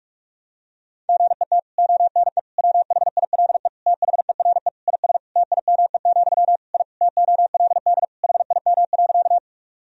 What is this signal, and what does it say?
Morse code, heard as a keyed tone, about 45 wpm, 700 Hz. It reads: GET ONE WHILE THERE IS TIME, I TOLD HIM.
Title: Sentences from Top 1000 Words - Review Part 1 - 45wpm